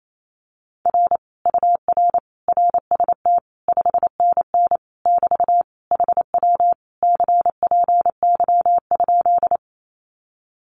Morse code, heard as a keyed tone, about 28 wpm, 700 Hz. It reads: R UR RST 5NN <BT> HW CPY?
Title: Common QSO Phrases Encore 28wpm